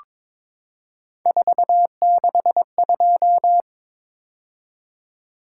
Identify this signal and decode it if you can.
Morse code, heard as a keyed tone, about 23 wpm, 700 Hz. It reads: 462
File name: practice / 22wpm / Three Letter-Number Combinations - 22wpm